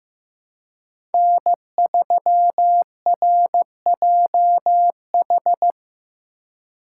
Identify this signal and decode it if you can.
Morse code, heard as a keyed tone, about 15 wpm, 700 Hz. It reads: N3RJH
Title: USA Call Signs - 5 Characters - 4x Character Spacing - 15wpm